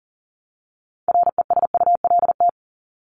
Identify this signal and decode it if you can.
Morse code, heard as a keyed tone, about 40 wpm, 700 Hz. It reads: RESULT
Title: ING Suffix - Top 100 Words with Base Words 40wpm